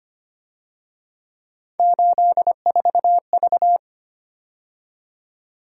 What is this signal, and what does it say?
Morse code, heard as a keyed tone, about 29 wpm, 700 Hz. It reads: 84V
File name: Speed-Racing - Three Letter-Number Combinations - 25wpm